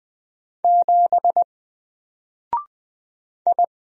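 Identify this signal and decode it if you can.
Morse code, heard as a keyed tone, about 20 wpm, 700 Hz. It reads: 7 I
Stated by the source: Single letter and number Morse drill at 20 wpm, from N4LCD.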